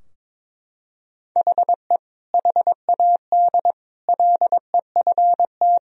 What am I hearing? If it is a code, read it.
Morse code, heard as a keyed tone, about 22 wpm, 700 Hz. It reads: HE HAD LEFT
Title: Sentences from Top 1000 Words - Review Part 3 - 22wpm